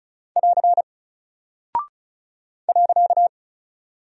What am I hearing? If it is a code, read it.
Morse code, heard as a keyed tone, about 36 wpm, 700 Hz. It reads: <AR> .
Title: Single Character (Letter:Number:Punctuation:Prosign) 35wpm